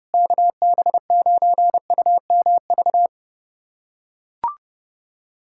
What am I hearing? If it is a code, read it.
Morse code, heard as a keyed tone, about 30 wpm, 690 Hz. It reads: KB9UMV
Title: USA Callsign DE Callsign 30wpm